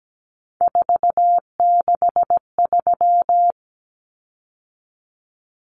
Morse code, heard as a keyed tone, about 17 wpm, 700 Hz. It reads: 463
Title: Common Cut Numbers - 3-Digit 17wpm